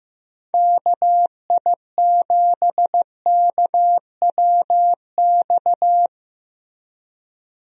Morse code, heard as a keyed tone, about 15 wpm, 700 Hz. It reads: KI7KWX
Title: USA Call Signs - 6 Characters - Repeated 3x - 15wpm